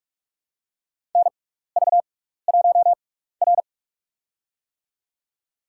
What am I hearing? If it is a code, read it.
Morse code, heard as a keyed tone, about 46 wpm, spaced as for 16 wpm, 700 Hz. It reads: NV1R
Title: USA Call Signs - 4 Characters - 6x Character Spacing - 45wpm